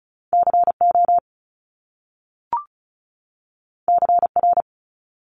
Morse code, heard as a keyed tone, about 36 wpm, 700 Hz. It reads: CO CR
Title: Top Two Letter Combinations in Words 35wpm